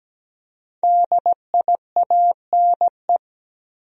Morse code, heard as a keyed tone, about 17 wpm, 700 Hz. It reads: DIANE
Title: USA Common Names - Max 5 Letters - 17wpm